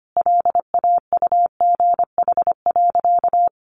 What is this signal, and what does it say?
Morse code, heard as a keyed tone, about 25 wpm, 700 Hz. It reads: LAUGH.
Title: Sentences from Top 1100 Words 25wpm